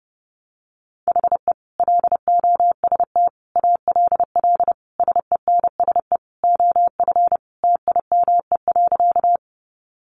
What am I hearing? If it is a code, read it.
Morse code, heard as a keyed tone, about 30 wpm, 700 Hz. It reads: HE LOST ALL SENSE OF TIME.
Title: Sentences from Top 1400 Words 30wpm